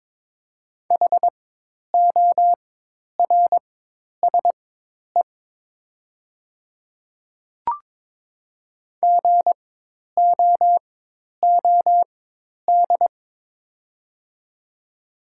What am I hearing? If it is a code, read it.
Morse code, heard as a keyed tone, about 22 wpm, spaced as for 10 wpm, 700 Hz. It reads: HORSE GOOD
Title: 4x Character Spacing - Top 300 Words - 22wpm